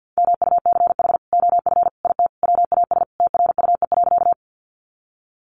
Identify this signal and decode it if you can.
Morse code, heard as a keyed tone, about 50 wpm, 700 Hz. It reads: MUCH OF IT WAS TRUE.
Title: Sentences from Top 400 Words 50wpm